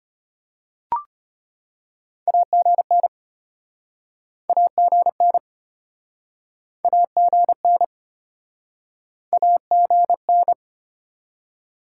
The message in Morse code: AGN AGN AGN AGN

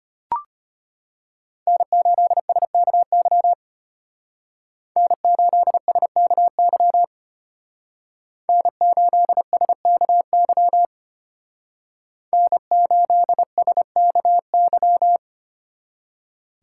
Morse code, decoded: N8SKY N8SKY N8SKY N8SKY

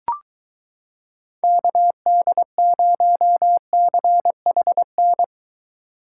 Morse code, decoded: KD0CHN

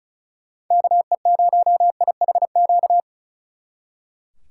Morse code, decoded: KE0IHQ

